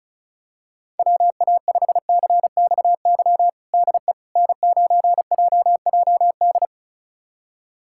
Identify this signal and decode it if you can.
Morse code, heard as a keyed tone, about 34 wpm, 700 Hz. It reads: WA5CXY DE N9JJD
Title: USA Callsign DE Callsign 35wpm